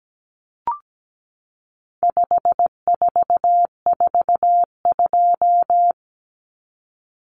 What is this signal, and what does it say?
Morse code, heard as a keyed tone, about 17 wpm, 700 Hz. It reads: 5442
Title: Numbers - 4-Digit 17wpm